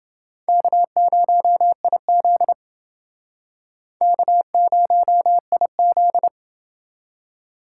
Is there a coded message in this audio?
Yes: Morse code from 0.5 to 6.3 s, heard as a keyed tone, about 28 wpm, 700 Hz. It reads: K0IZ K0IZ